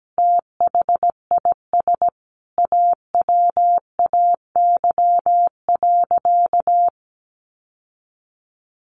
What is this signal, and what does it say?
Morse code, heard as a keyed tone, about 17 wpm, 700 Hz. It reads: THIS AWAY.
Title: Sentences from Top 1100 Words 17wpm